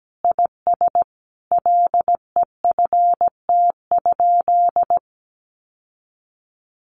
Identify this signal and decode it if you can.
Morse code, heard as a keyed tone, about 17 wpm, 700 Hz. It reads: IS LEFT?
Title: Sentences from Top 1300 Words 17wpm